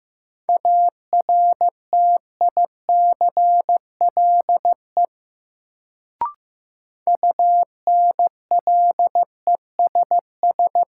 Morse code, heard as a keyed tone, about 15 wpm, 700 Hz. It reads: ARTICLE UNLESS
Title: Sentences from Top 1000 Words - Review Part 5 - New Word Drill - 15wpm